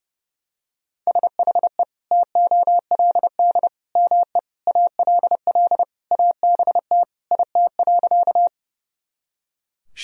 Morse code, heard as a keyed tone, about 30 wpm, 700 Hz. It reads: SHE TOLD ME ALL ABT IT.
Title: Sentences from Top 400 Words and Abbreviations 30wpm